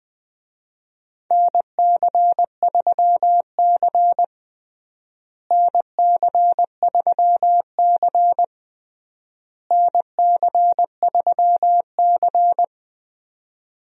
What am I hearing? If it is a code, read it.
Morse code, heard as a keyed tone, about 20 wpm, 700 Hz. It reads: NC3C NC3C NC3C
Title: USA Call Signs - 4 Characters - Repeated 3x - 20wpm